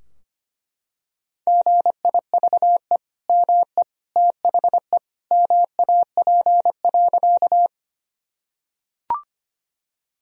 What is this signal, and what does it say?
Morse code, heard as a keyed tone, about 25 wpm, 700 Hz. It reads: GIVE ME THE MAP.
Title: Sentences from Top 1000 Words - Review Part 2 - 25wpm